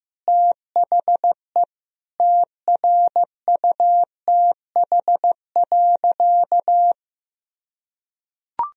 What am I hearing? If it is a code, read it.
Morse code, heard as a keyed tone, about 15 wpm, 700 Hz. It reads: THE TRUTH.
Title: Sentences from Top 1000 Words - Review Part 2 - 15wpm